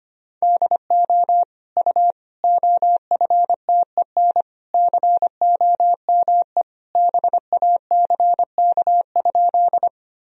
Morse code, heard as a keyed tone, about 25 wpm, 700 Hz. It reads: DO U OFTEN COME BACK?
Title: Sentences from Top 300 Words and Abbreviations 25wpm